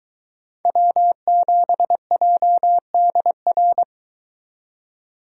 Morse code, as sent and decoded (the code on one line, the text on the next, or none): .-- --... .--- -.. .-.
W7JDR